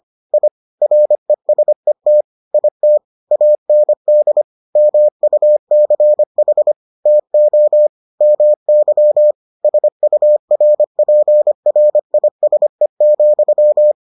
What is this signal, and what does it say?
Morse code, heard as a keyed tone, about 25 wpm, 600 Hz. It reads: I RESET IT AND MUCH TO MY SURPRISE,